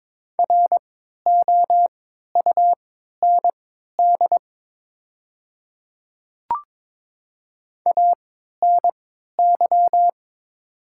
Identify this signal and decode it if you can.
Morse code, heard as a keyed tone, about 22 wpm, spaced as for 13 wpm, 700 Hz. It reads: ROUND ANY
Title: 3x Character Spacing - Top 300 Words - 22wpm